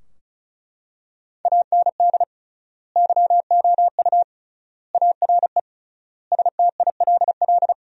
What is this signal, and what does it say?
Morse code, heard as a keyed tone, about 36 wpm, 700 Hz. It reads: AND YOU ARE STILL
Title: 3x Word-Spacing - Sentences from Top 400 Words - 35wpm